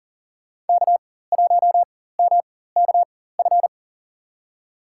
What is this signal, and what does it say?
Morse code, heard as a keyed tone, about 40 wpm, spaced as for 19 wpm, 700 Hz. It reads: K1MKF
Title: USA Call Signs - 4x Character Spacing - 40wpm